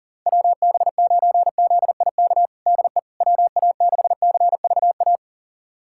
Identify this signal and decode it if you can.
Morse code, heard as a keyed tone, about 37 wpm, 700 Hz. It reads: WB9ZIK DE WA6CVA
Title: USA Callsign DE Callsign - 6 Characters - 40wpm